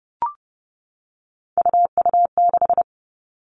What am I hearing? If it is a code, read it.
Morse code, heard as a keyed tone, about 30 wpm, 700 Hz. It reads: UU6
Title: Three Challenging Letter-Number Combinations 30wpm